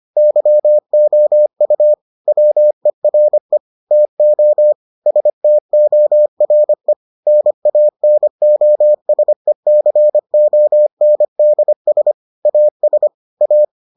Morse code, heard as a keyed tone, about 25 wpm, 600 Hz. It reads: YOU WERE TO STORE NANOSECONDS AS A